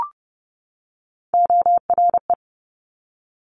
Morse code, heard as a keyed tone, about 30 wpm, 690 Hz. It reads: ORE
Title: Top Three Letter Combinations in Words 30wpm